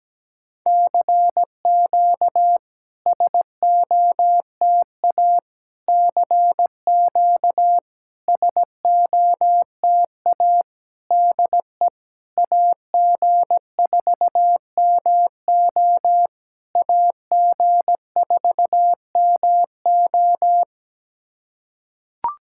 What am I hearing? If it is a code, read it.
Morse code, heard as a keyed tone, about 17 wpm, 700 Hz. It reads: CQ SOTA CQ SOTA DE AG4MO AG4MO
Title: SOTA QSO Practice 17wpm